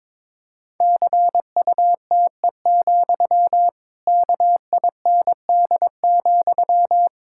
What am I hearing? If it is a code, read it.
Morse code, heard as a keyed tone, about 22 wpm, 700 Hz. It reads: CUTE, KIND,
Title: Pangram Challenge v2 - 22wpm